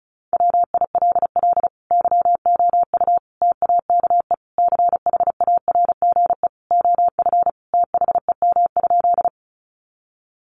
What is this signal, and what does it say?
Morse code, heard as a keyed tone, about 34 wpm, 700 Hz. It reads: WILL YOU TAKE CHARGE OF THEM?
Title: Sentences from Top 900 Words 35wpm